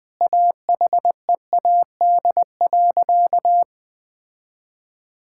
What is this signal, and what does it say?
Morse code, heard as a keyed tone, about 20 wpm, 700 Hz. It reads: AHEAD.
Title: Sentences with 2-Word Phrases - 20wpm